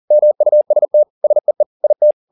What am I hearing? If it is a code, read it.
Morse code, heard as a keyed tone, about 40 wpm, 600 Hz. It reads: MUST SEE IT